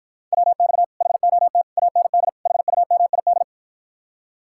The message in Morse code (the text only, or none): WX HOT AND HUMID